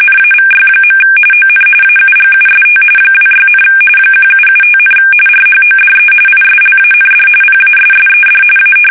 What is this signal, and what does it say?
RTTY, about 45.45 baud. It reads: NAVY. ON AUGUST 26, 1775, RHODE ISLAND PASSED A
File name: fsk850 / usnavy1